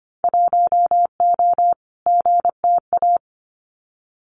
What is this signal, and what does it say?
Morse code, heard as a keyed tone, about 24 wpm, 700 Hz. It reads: 1O GTA
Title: Winter Field Day Exchanges 25wpm